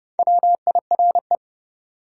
Morse code, heard as a keyed tone, about 29 wpm, 700 Hz. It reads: WIRE
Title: Top 180 QSO Elements 30wpm